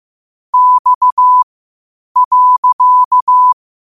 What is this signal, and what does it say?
Morse code, heard as a keyed tone, about 15 wpm, spaced as for 8 wpm, 1000 Hz. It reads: X.
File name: qotdfiles / 8wpm